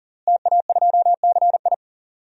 Morse code, heard as a keyed tone, about 40 wpm, 700 Hz. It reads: TA2CI
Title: DXCC-100 Call Signs - 5 Characters - Repeated 2x - 40wpm